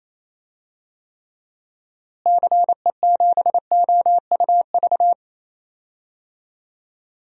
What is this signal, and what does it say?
Morse code, heard as a keyed tone, about 28 wpm, 700 Hz. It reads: CE7OUV
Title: DXCC-100 Call Signs - 6 Characters - Repeated 3x - 28wpm